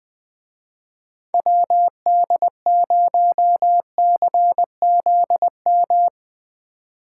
Morse code, heard as a keyed tone, about 20 wpm, 700 Hz. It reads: WD0CZM